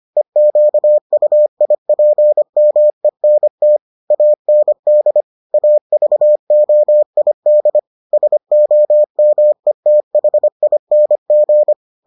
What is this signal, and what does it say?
Morse code, heard as a keyed tone, about 25 wpm, 600 Hz. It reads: EQUIPMENT AND AVOID SOMETHING